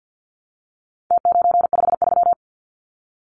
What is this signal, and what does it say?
Morse code, heard as a keyed tone, about 50 wpm, 700 Hz. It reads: T953
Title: Numbers - Contest Style - 4-Digit 50wpm